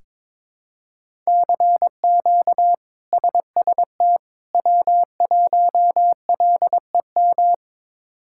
Morse code, heard as a keyed tone, about 22 wpm, 700 Hz. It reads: CQ SST W1LEM
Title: K1USN SST QSO Practice - 22wpm